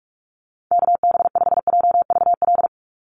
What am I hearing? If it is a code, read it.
Morse code, heard as a keyed tone, about 44 wpm, 700 Hz. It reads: KB5JVL
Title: USA Call Signs 45wpm